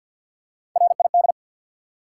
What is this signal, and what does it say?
Morse code, heard as a keyed tone, about 51 wpm, 700 Hz. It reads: RID
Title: Three Letter Words - 50wpm